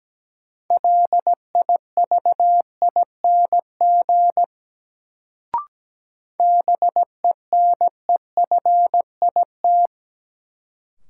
Morse code, heard as a keyed tone, about 17 wpm, 700 Hz. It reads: LIVING BENEFIT
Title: Sentences from Top 1000 Words - Review Part 6 - New Word Drill - 17wpm